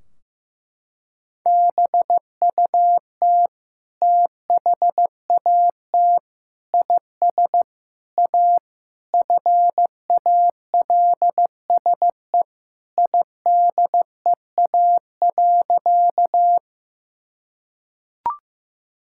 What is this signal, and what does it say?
Morse code, heard as a keyed tone, about 15 wpm, 700 Hz. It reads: BUT THAT IS A FALSE IDEA.
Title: Sentences from Top 1600 Words 15wpm